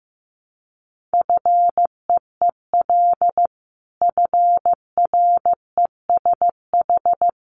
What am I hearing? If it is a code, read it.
Morse code, heard as a keyed tone, about 15 wpm, 700 Hz. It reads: FEEL FRESH
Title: Sentences from Top 1200 Words 15wpm